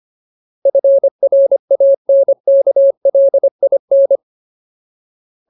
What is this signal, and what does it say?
Morse code, heard as a keyed tone, about 25 wpm, 550 Hz. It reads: FRANKLIN
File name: Ultimate Word List / 25WPM Ultimate